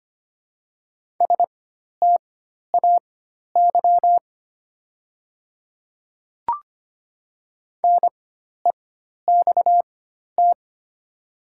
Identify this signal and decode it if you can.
Morse code, heard as a keyed tone, about 25 wpm, spaced as for 12 wpm, 700 Hz. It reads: STAY NEXT
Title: 4x Character Spacing - Top 500 Words - 25wpm